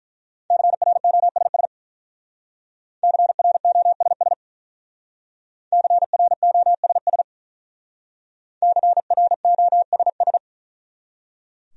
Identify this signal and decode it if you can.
Morse code, heard as a keyed tone, about 43 wpm, 700 Hz. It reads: CROSS CROSS CROSS CROSS
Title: Speed-Racing Top 500 Words 35wpm